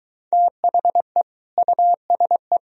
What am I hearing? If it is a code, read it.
Morse code, heard as a keyed tone, about 23 wpm, 700 Hz. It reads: THE USE